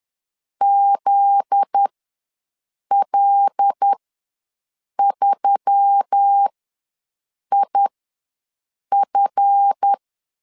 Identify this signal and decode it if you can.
Morse code, heard as a keyed tone, about 10 wpm, spaced as for 6 wpm, 800 Hz. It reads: ZL3IF
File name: LEZIONI / 7 GF9Z / 7 GF9Z ws 10wpm 3919